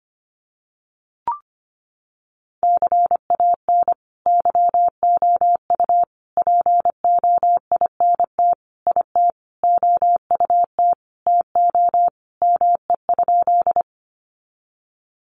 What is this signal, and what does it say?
Morse code, heard as a keyed tone, about 25 wpm, 700 Hz. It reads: CAN YOU POINT IT OUT TO ME?
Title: Sentences from Top 200 Words 25wpm